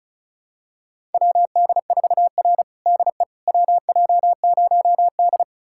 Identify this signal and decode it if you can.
Morse code, heard as a keyed tone, about 34 wpm, 700 Hz. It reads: WD4R DE WJ0D
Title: USA Callsign DE Callsign - 4 Characters - 35wpm